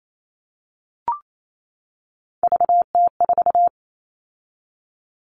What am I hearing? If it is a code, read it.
Morse code, heard as a keyed tone, about 28 wpm, 700 Hz. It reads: VT4
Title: All Cut Numbers - 3-Digit 28wpm